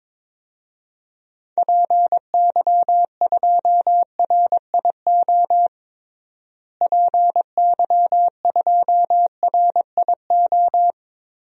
Morse code, heard as a keyed tone, about 22 wpm, 700 Hz. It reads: PY2RIO PY2RIO
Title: DXCC-100 Call Signs - 6 Characters - Repeated 3x - 22wpm